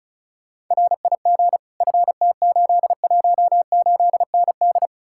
Morse code, heard as a keyed tone, about 46 wpm, 700 Hz. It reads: RIG FT818ND